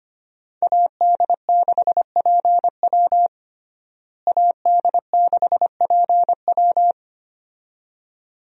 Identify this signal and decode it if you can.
Morse code, heard as a keyed tone, about 25 wpm, 700 Hz. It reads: AD6PW AD6PW
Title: USA Call Signs - 5 Characters - Repeated 2x - 25wpm